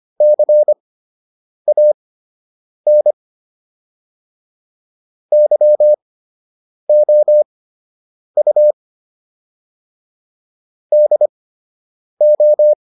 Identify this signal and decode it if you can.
Morse code, heard as a keyed tone, about 25 wpm, spaced as for 8 wpm, 600 Hz. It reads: CAN YOU DO